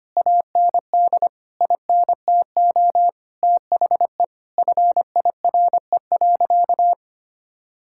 Morse code, heard as a keyed tone, about 25 wpm, 700 Hz. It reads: AND INTO THE FIRE.
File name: practice / 25wpm / Sentences with 9-Word Phrases - 25wpm